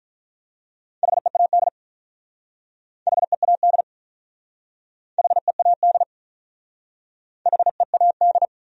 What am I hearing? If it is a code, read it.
Morse code, heard as a keyed tone, about 48 wpm, 700 Hz. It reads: HEAD HEAD HEAD HEAD